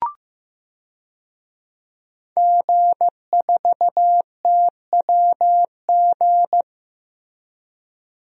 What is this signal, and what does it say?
Morse code, heard as a keyed tone, about 15 wpm, 700 Hz. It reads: G4TWG
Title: DXCC-100 Call Signs - 5 Characters - Repeated 2x - 15wpm